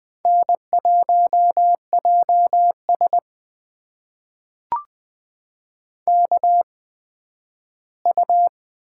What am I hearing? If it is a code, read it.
Morse code, heard as a keyed tone, about 20 wpm, spaced as for 12 wpm, 700 Hz. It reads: N1JS KU
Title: USA Call Signs - 4 Characters - 8x Character Spacing - 20wpm